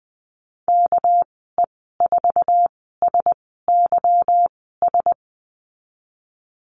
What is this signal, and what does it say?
Morse code, heard as a keyed tone, about 20 wpm, spaced as for 14 wpm, 700 Hz. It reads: KE4SYS
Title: 231 Lesson 37g - Letter x - Call Signs 20wpm